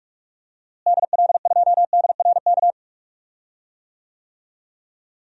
Morse code, heard as a keyed tone, about 46 wpm, 700 Hz. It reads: DL2DRK
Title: DXCC-100 Call Signs - 6 Characters - Repeated 2x - 45wpm